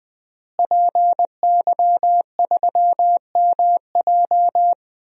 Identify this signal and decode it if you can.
Morse code, heard as a keyed tone, about 21 wpm, 700 Hz. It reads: PY3MJ
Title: DXCC-100 Call Signs - 5 Characters - Repeated 2x - 20wpm